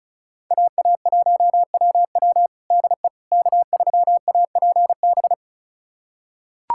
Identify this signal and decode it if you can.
Morse code, heard as a keyed tone, about 34 wpm, 700 Hz. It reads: AA1WW DE K3APB
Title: USA Callsign DE Callsign - 5 Characters - 35wpm